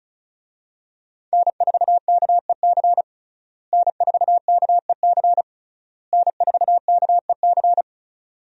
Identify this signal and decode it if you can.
Morse code, heard as a keyed tone, about 36 wpm, 700 Hz. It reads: N4KEC N4KEC N4KEC
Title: USA Call Signs - 5 Characters - Repeated 3x - 35wpm